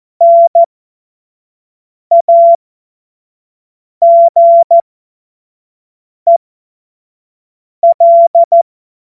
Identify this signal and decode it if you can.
Morse code, heard as a keyed tone, about 14 wpm, spaced as for 5 wpm, 680 Hz. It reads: NAGEL
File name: Morsekurs / Witze / Witz 02 Januar 2024 25 F70